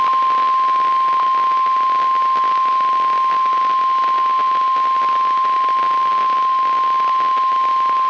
Emergency Alert System (SAME) tones (the 1050 Hz attention tone), with noise. It is heard from 0.3 to 7.8 s.
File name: eas messages / EAS WXL38 Binghamton